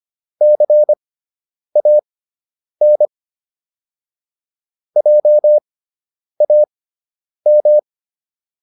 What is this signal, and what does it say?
Morse code, heard as a keyed tone, about 25 wpm, spaced as for 9 wpm, 600 Hz. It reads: CAN JAM